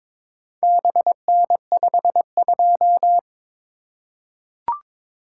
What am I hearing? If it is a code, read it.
Morse code, heard as a keyed tone, about 22 wpm, 700 Hz. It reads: BN52
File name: practice / 22wpm / Four Letter-Number Combinations - 22wpm